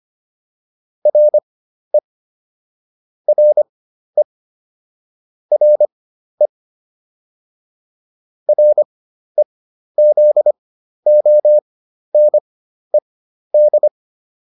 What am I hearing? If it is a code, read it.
Morse code, heard as a keyed tone, about 25 wpm, spaced as for 12 wpm, 600 Hz. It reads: RE RE RE REZONED